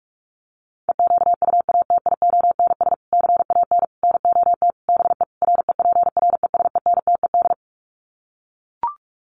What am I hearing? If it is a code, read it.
Morse code, heard as a keyed tone, about 44 wpm, 700 Hz. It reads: EQUATIONS CAN NOT BE REPRESENTED
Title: Sets of 5 Words 45wpm